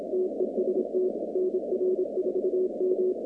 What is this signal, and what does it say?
Morse code, heard as a keyed tone, about 27 wpm, 370 Hz, with noise. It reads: TEST NR4M
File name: 2019-cw-wpx-kt5j-k5pi